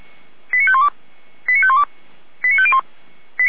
Five-tone selective call (ZVEI), heard as quick runs of five tones, tones of about 70 ms. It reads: 89512 89512 89501
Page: ZVEI